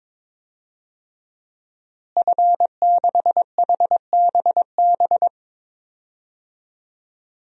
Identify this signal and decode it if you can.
Morse code, heard as a keyed tone, about 22 wpm, 700 Hz. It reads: F6HBB
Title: DXCC-100 Call Signs - 5 Characters - Repeated 2x - 22wpm